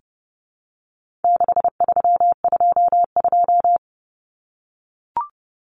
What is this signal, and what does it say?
Morse code, heard as a keyed tone, about 30 wpm, 700 Hz. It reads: 6322
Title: Numbers - 4-Digit 30wpm